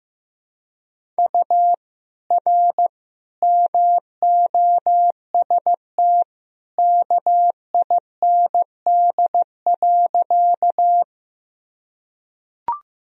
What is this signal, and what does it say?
Morse code, heard as a keyed tone, about 15 wpm, 700 Hz. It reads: U R MOST KIND.
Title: Sentences from Top 200 Words and Abbreviations 15wpm